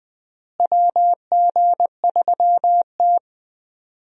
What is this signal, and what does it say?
Morse code, heard as a keyed tone, about 20 wpm, 700 Hz. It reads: WG3T
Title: USA Call Signs - 4 Characters - Repeated 3x - 20wpm